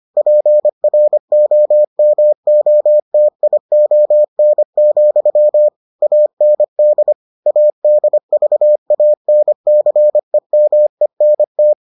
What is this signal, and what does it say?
Morse code, heard as a keyed tone, about 25 wpm, 600 Hz. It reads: PROMOTION, AND ADVANCEMENT